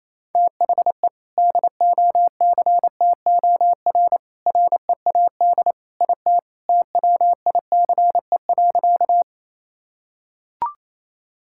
Morse code, heard as a keyed tone, about 28 wpm, 700 Hz. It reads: THE DOCTOR READ IT TWICE.
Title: Sentences from Top 1600 Words 28wpm